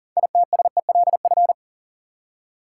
Morse code, heard as a keyed tone, about 40 wpm, 700 Hz. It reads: ITSELF
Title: Six Letter Words - 40wpm